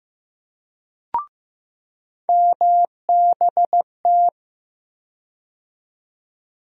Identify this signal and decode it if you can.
Morse code, heard as a keyed tone, about 15 wpm, 700 Hz. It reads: MBT
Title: USA Call Sign Postfixes - 3 Letters 15wpm